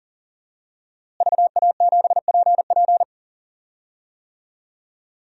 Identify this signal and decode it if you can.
Morse code, heard as a keyed tone, about 38 wpm, 700 Hz. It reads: VA7PP
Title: DXCC-100 Call Signs - 5 Characters - Repeated 3x - 40wpm